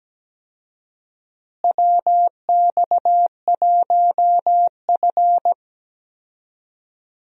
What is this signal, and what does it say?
Morse code, heard as a keyed tone, about 17 wpm, 700 Hz. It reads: WX1F
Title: USA Call Signs - 4 Characters - Repeated 3x - 17wpm